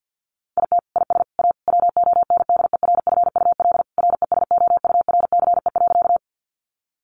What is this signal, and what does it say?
Morse code, heard as a keyed tone, about 50 wpm, 700 Hz. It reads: IT IS A WONDERFUL RESOURCE.
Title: Sentences from Top 1300 Words 50wpm